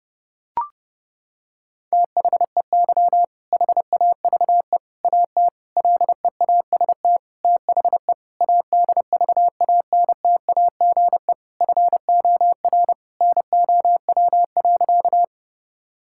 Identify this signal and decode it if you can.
Morse code, heard as a keyed tone, about 30 wpm, 700 Hz. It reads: THEY HAVE AT LEAST THE ADVANTAGE FOR NOW.
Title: Sentences from Top 1000 Words - Review Part 6 - 30wpm